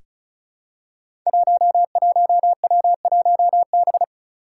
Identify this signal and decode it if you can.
Morse code, heard as a keyed tone, about 34 wpm, 700 Hz. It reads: 11W1B
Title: Five Challenging Letter-Number Combinations 35wpm